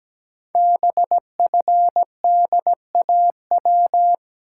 Morse code, heard as a keyed tone, about 17 wpm, 700 Hz. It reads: BFDAW
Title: Code Groups - 5 Letters - 17wpm